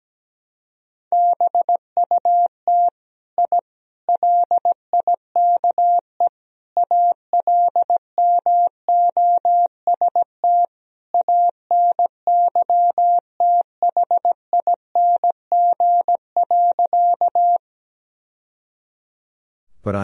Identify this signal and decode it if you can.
Morse code, heard as a keyed tone, about 17 wpm, 700 Hz. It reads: BUT I LIKE ALMOST ANYTHING.
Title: Sentences from Top 1000 Words - Review Part 4 - 17wpm